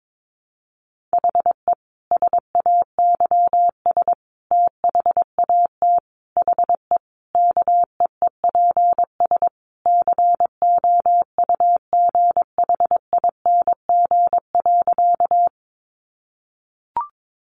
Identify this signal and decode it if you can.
Morse code, heard as a keyed tone, about 22 wpm, 700 Hz. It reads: HE SAYS THAT HE KEEPS COUGHING.
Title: Additional Easy Phrases 22wpm